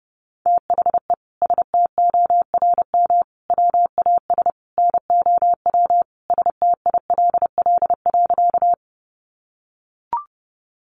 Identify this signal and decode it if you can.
Morse code, heard as a keyed tone, about 30 wpm, 700 Hz. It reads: THE STORM WAS NOW STILL.
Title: Sentences from Top 1500 Words 30wpm